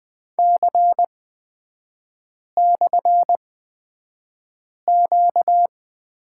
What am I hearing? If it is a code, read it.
Morse code, heard as a keyed tone, about 20 wpm, spaced as for 5 wpm, 700 Hz. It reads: C/Q
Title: 236 Lesson 38e - Letter q - Three Characters 20wpm